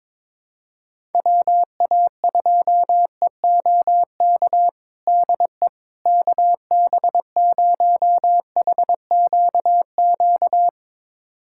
Morse code, heard as a keyed tone, about 21 wpm, 700 Hz. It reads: WA2EOK DE KB0HQQ
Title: USA Callsign DE Callsign - 6 Characters - 22wpm